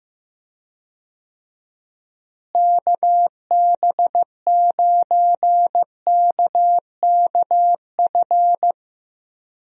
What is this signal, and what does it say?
Morse code, heard as a keyed tone, about 15 wpm, 700 Hz. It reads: KB9KKF
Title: USA Call Signs - 6 Characters - Repeated 2x - 15wpm